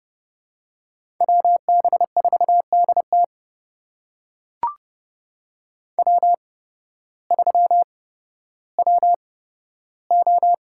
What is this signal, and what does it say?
Morse code, heard as a keyed tone, about 30 wpm, spaced as for 14 wpm, 700 Hz. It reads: WB4DT W3WO
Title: USA Call Signs - 5 Characters - 8x Character Spacing - 30wpm